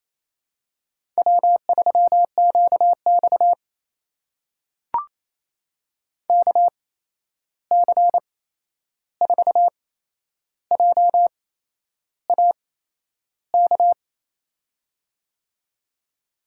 Morse code, heard as a keyed tone, about 27 wpm, spaced as for 10 wpm, 700 Hz. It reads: W3QX KC4JAK
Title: USA Call Signs - 8x Character Spacing - 28wpm